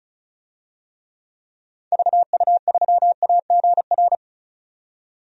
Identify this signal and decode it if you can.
Morse code, heard as a keyed tone, about 34 wpm, 700 Hz. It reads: VU3AGR
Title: DXCC-100 Call Signs - 6 Characters - 35wpm